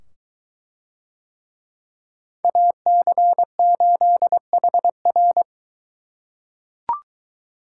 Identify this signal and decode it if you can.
Morse code, heard as a keyed tone, about 23 wpm, 700 Hz. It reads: AC8HR